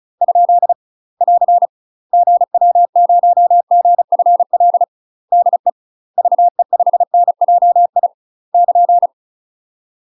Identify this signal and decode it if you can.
Morse code, heard as a keyed tone, about 36 wpm, 700 Hz. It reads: ? <AR> GW0GFL DE VE5NJI <KN>